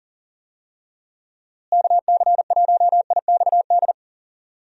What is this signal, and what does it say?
Morse code, heard as a keyed tone, about 40 wpm, 700 Hz. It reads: KC1IXD
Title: USA Call Signs - 6 Characters - Repeated 3x - 40wpm